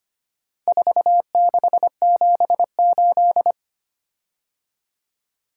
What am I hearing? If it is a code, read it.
Morse code, heard as a keyed tone, about 25 wpm, 700 Hz. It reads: 4678
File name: practice / 25wpm / Four Letter-Number Combinations - 25wpm